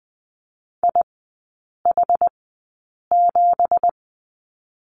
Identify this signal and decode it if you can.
Morse code, heard as a keyed tone, about 20 wpm, spaced as for 8 wpm, 700 Hz. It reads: IH7
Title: 178 Lesson 29e - Slash - Three Characters 20wpm